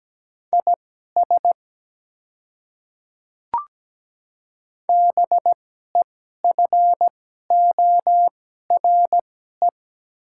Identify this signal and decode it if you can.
Morse code, heard as a keyed tone, about 17 wpm, spaced as for 12 wpm, 700 Hz. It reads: IS BEFORE
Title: 2x Character Spacing - Top 200 Words - 17wpm